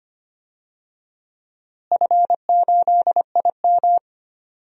Morse code, heard as a keyed tone, about 25 wpm, 700 Hz. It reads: F8IM
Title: DXCC-100 Call Signs - 4 Characters - Repeated 3x - 25wpm